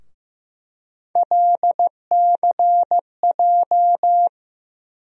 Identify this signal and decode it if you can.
Morse code, heard as a keyed tone, about 15 wpm, 700 Hz. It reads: LCJ
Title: USA Call Sign Postfixes - 3 Letters 15wpm